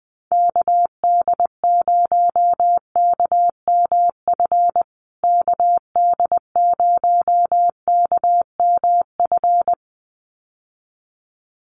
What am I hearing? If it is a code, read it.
Morse code, heard as a keyed tone, about 19 wpm, 700 Hz. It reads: KD0KMF KD0KMF
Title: USA Call Signs Repeated 2x 20wpm